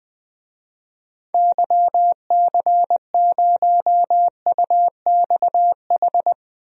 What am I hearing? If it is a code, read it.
Morse code, heard as a keyed tone, about 20 wpm, 700 Hz. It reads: YC0UXH
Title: DXCC-100 Call Signs - 6 Characters - 20wpm